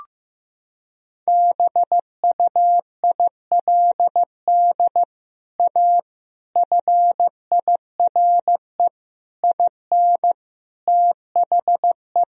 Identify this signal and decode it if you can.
Morse code, heard as a keyed tone, about 15 wpm, 700 Hz. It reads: BUILD A FIRE IN THE
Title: Sentences from Top 1000 Words - Review Part 4 - 15wpm